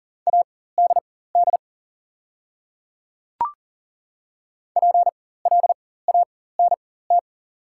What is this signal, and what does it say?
Morse code, heard as a keyed tone, about 38 wpm, spaced as for 18 wpm, 690 Hz. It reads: ADD PLANT
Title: 4x Character Spacing - Top 300 Words - 40wpm